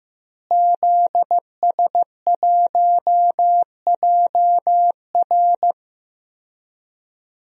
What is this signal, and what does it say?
Morse code, heard as a keyed tone, about 15 wpm, 700 Hz. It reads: ZS1JR
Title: DXCC-100 Call Signs - 5 Characters - Repeated 3x - 15wpm